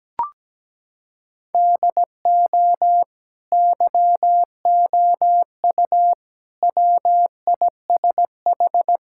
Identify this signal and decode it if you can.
Morse code, heard as a keyed tone, about 17 wpm, 700 Hz. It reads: DO YOU WISH
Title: Sentences from Top 1600 Words 17wpm